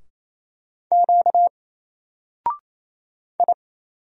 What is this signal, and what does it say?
Morse code, heard as a keyed tone, about 27 wpm, 700 Hz. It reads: Q I
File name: Single-Letter-Number-28wpm